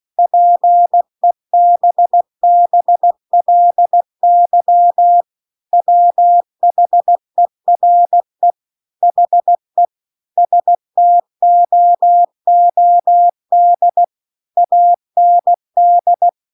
Morse code, heard as a keyed tone, about 16 wpm, 700 Hz. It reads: PEBBLY WHERE HE STOOD AND
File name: CW-The-Big-Twoheaded-River-2-16WPM